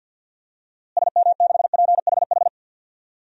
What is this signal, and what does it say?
Morse code, heard as a keyed tone, about 50 wpm, 700 Hz. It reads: SM6PHH
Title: DXCC-100 Call Signs - 6 Characters - Repeated 2x - 50wpm